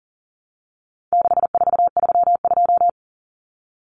Morse code, heard as a keyed tone, about 40 wpm, 700 Hz. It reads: 6432
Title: Common Cut Numbers - 4-Digit 40wpm